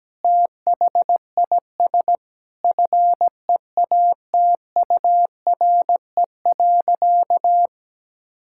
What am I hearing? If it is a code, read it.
Morse code, heard as a keyed tone, about 17 wpm, 700 Hz. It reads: THIS FEATURE.
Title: Sentences from Top 1000 Words - Review Part 5 - 17wpm